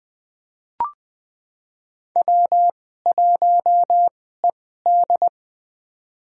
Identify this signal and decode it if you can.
Morse code, heard as a keyed tone, about 20 wpm, spaced as for 14 wpm, 700 Hz. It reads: W1ED
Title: USA Call Signs - 4 Characters - 2x Character Spacing - 20wpm